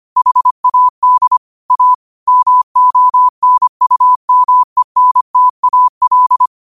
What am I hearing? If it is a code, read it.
Morse code, heard as a keyed tone, about 25 wpm, 1000 Hz. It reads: HAD A MONUMENTAL
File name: qotdfiles / 25wpm